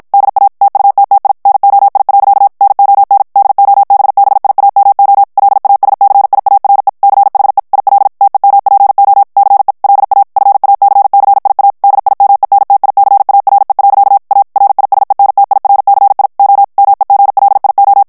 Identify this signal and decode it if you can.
Morse code, heard as a keyed tone, about 54 wpm, 800 Hz. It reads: DA TUTTI NOI. NON DOBBIAMO LASCIARE CHE IL TEMPO CE LA FACCIA DIMENTICARE. A RISENTIRCI O MEGLIO
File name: ITALIEN texte clair 1 vitesse 50 mots minute